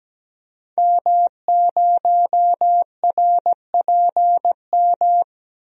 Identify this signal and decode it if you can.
Morse code, heard as a keyed tone, about 17 wpm, 700 Hz. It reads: M0RPM
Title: DXCC-100 Call Signs - 5 Characters - Repeated 3x - 17wpm